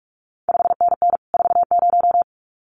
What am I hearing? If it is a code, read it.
Morse code, heard as a keyed tone, about 46 wpm, 700 Hz. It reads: 5NN 40
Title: CQ WW DX Contest Exchange Encore - 45wpm